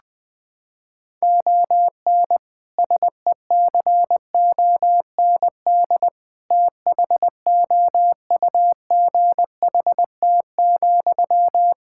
Morse code, heard as a keyed tone, about 20 wpm, 700 Hz. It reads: ON SECOND THOUGHT,
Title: Sentences from Top 1600 Words 20wpm